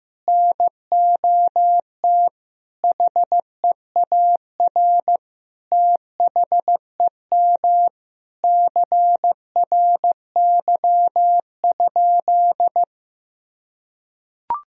Morse code, heard as a keyed tone, about 15 wpm, 700 Hz. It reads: NOT HEAR THEM CRY?
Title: Sentences from Top 1000 Words - Review Part 2 - 15wpm